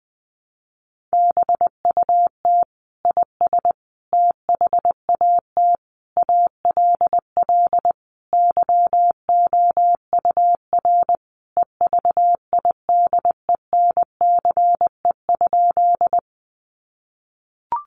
Morse code, heard as a keyed tone, about 20 wpm, 700 Hz. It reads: BUT IS THAT ALL YOUR EVIDENCE?
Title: Sentences from Top 1300 Words 20wpm